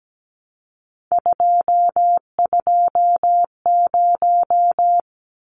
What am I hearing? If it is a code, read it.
Morse code, heard as a keyed tone, about 17 wpm, 700 Hz. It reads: 220
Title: Numbers - 3-Digit 17wpm